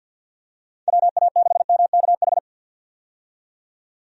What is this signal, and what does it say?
Morse code, heard as a keyed tone, about 53 wpm, 700 Hz. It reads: WA6MKH
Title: USA Call Signs - 6 Characters - 50wpm